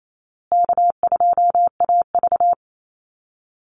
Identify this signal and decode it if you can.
Morse code, heard as a keyed tone, about 28 wpm, 700 Hz. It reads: K2AV
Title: Easy Call Signs Repeated 3x - 28wpm